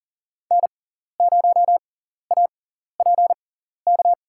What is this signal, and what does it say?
Morse code, heard as a keyed tone, about 40 wpm, spaced as for 14 wpm, 700 Hz. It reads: N0APK
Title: USA Call Signs - 6x Character Spacing - 40wpm